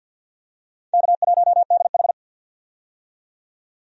Morse code, heard as a keyed tone, about 52 wpm, 700 Hz. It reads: K1DH
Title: USA Call Signs - 4 Characters - 50wpm